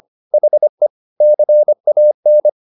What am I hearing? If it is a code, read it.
Morse code, heard as a keyed tone, about 25 wpm, 600 Hz. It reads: HE CAN
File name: the old man and the sea part 1 25wpm